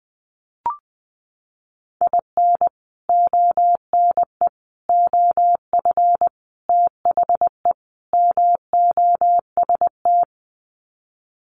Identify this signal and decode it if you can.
Morse code, heard as a keyed tone, about 20 wpm, 700 Hz. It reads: IN ONE OF THE MOST
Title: Sets of 5 Words Encore 20wpm